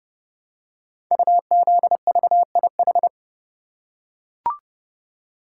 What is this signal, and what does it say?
Morse code, heard as a keyed tone, about 30 wpm, 700 Hz. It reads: UZVIH
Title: Code Groups - 5 Letters - 30wpm